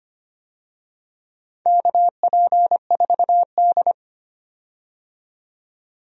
Morse code, heard as a keyed tone, about 25 wpm, 700 Hz. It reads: KP4D
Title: DXCC-100 Call Signs - 4 Characters - 25wpm